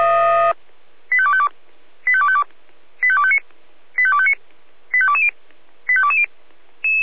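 Five-tone selective call (ZVEI), heard as quick runs of five tones, tones of about 70 ms. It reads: 85252 85252 85259 85259 85220 85220
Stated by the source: Five-tone ZVEI selective call received on SDR.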